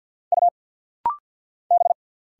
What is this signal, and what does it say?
Morse code, heard as a keyed tone, about 49 wpm, 700 Hz. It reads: U B
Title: Single Letter-Number Alt - Rapid-Fire 50wpm